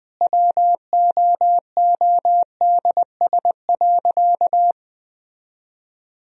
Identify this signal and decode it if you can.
Morse code, heard as a keyed tone, about 20 wpm, 700 Hz. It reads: WOODS.
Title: Sentences from Top 1000 Words - Review Part 2 - 20wpm